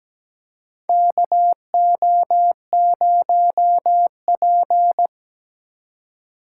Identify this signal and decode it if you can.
Morse code, heard as a keyed tone, about 17 wpm, 700 Hz. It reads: KO0P
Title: USA Call Signs - 4 Characters - 8x Character Spacing - 17wpm